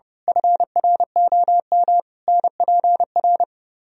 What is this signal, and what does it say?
Morse code, heard as a keyed tone, about 28 wpm, 700 Hz. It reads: FROM NPR
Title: Speed-Racing - Sets of 5 Words 30wpm